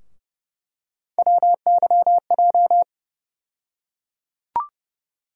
Morse code, heard as a keyed tone, about 31 wpm, 700 Hz. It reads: WYJ
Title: USA Call Sign Postfixes - 3 Letters 30wpm